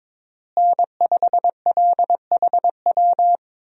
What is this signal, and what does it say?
Morse code, heard as a keyed tone, about 21 wpm, 700 Hz. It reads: N5LHW
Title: USA Call Signs - 5 Characters - Repeated 3x - 22wpm